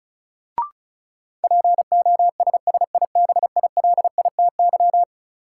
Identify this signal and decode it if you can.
Morse code, heard as a keyed tone, about 36 wpm, 700 Hz. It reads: POSSIBILITY
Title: Sets of 3 Words - New Word Drill - 35wpm